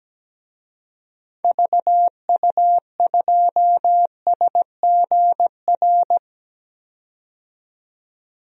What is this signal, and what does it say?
Morse code, heard as a keyed tone, about 17 wpm, 700 Hz. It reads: VU2SGR
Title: DXCC-100 Call Signs - 6 Characters - 17wpm